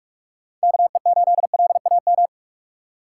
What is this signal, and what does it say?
Morse code, heard as a keyed tone, about 46 wpm, 700 Hz. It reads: K E 8 L A M
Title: USA Call Signs - 6 Characters - 6x Character Spacing - 45wpm